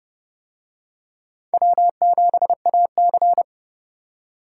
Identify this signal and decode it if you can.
Morse code, heard as a keyed tone, about 29 wpm, 700 Hz. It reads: W7AC